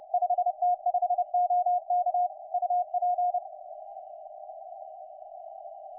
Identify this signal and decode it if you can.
Morse code, heard as a keyed tone, about 30 wpm, 720 Hz. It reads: 5T5OK UP
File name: 5T5OK 10CW